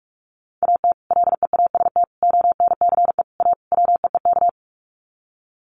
Morse code, heard as a keyed tone, about 44 wpm, 700 Hz. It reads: AT LEAST ONCE A WEEK
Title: Sets of 5 Words Encore2 45wpm